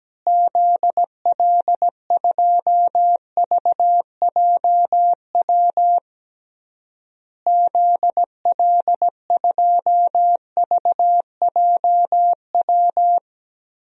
Morse code, heard as a keyed tone, about 17 wpm, 700 Hz. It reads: ZL2VJW ZL2VJW
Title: DXCC-100 Call Signs - 6 Characters - Repeated 3x - 17wpm